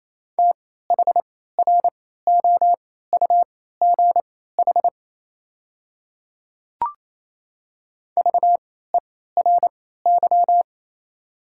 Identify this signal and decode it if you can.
Morse code, heard as a keyed tone, about 27 wpm, spaced as for 16 wpm, 700 Hz. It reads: THROUGH VERY
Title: 3x Character Spacing - Top 200 Words - 28wpm